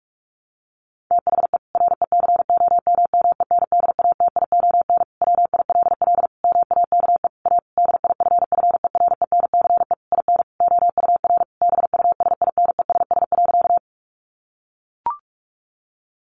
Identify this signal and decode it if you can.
Morse code, heard as a keyed tone, about 44 wpm, 700 Hz. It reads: THE RECOMMENDATION WILL MAKE A DIFFERENCE IN OUR BUSINESS.
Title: Sentences from Top 1400 Words 45wpm